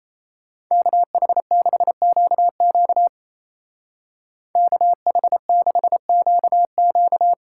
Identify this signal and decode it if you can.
Morse code, heard as a keyed tone, about 33 wpm, 700 Hz. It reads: KH6QQ KH6QQ
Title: Speed Racing - USA 2x2 Call Signs 28wpm